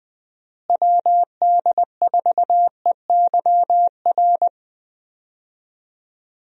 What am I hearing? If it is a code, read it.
Morse code, heard as a keyed tone, about 20 wpm, 700 Hz. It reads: WD4EYR
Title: USA Call Signs - 6 Characters - Repeated 3x - 20wpm